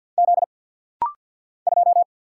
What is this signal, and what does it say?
Morse code, heard as a keyed tone, about 49 wpm, 700 Hz. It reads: Z 2
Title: Single Letter-Number - Rapid-Fire 50wpm